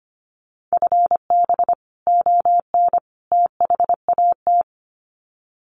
Morse code, heard as a keyed tone, about 25 wpm, 700 Hz. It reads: FB ON THAT
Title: Common QSO Phrases 25wpm